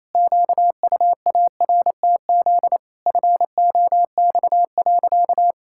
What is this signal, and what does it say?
Morse code, heard as a keyed tone, about 28 wpm, 700 Hz. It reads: QUARTZ FOX.
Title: Pangram Challenge v2 - 28wpm